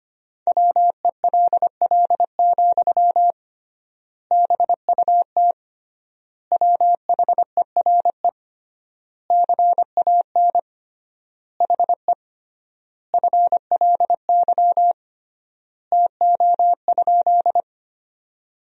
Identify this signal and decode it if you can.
Morse code, heard as a keyed tone, about 25 wpm, 700 Hz. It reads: WELL, BUT WHERE CAN HE FLY TO?
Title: 3x Word-Spacing - Sentences from Top 500 Words - 25wpm